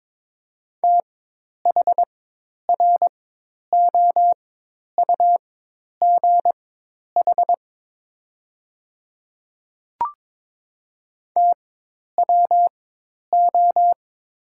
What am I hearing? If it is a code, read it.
Morse code, heard as a keyed tone, about 22 wpm, spaced as for 10 wpm, 690 Hz. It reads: THROUGH TWO
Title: 4x Character Spacing - Top 200 Words - 22wpm